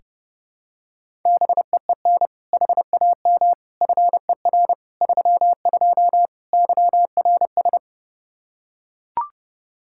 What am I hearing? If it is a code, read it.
Morse code, heard as a keyed tone, about 30 wpm, 700 Hz. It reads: BEEN HAM FER 32 YRS
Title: K1USN SST QSO Practice 30wpm